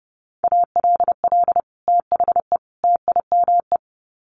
Morse code, heard as a keyed tone, about 30 wpm, 700 Hz. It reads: ALL THE TIME
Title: Sets of 3 Words 30wpm